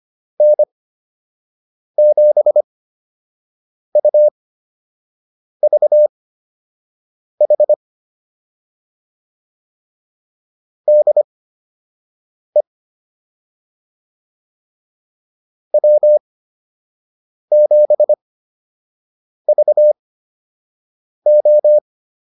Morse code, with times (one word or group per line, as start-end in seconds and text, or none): 0.4-7.7 N7UVH
10.9-12.6 DE
15.7-21.8 W7VO